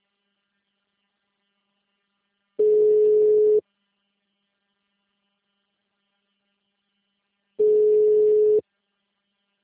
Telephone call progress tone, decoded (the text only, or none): ringback tone